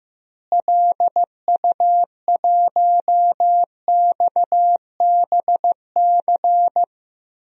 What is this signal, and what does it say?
Morse code, heard as a keyed tone, about 15 wpm, 700 Hz. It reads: LU1XBC
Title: DXCC-100 Call Signs - 6 Characters - Repeated 3x - 15wpm